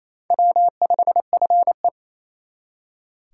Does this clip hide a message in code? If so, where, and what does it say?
Yes: Morse code from 0.3 to 1.9 s, heard as a keyed tone, about 36 wpm, 700 Hz. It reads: W5FE